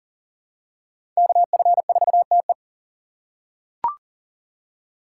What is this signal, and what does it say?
Morse code, heard as a keyed tone, about 40 wpm, 700 Hz. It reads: KF4TE
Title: USA Call Signs - 5 Characters - 2x Character Spacing - 40wpm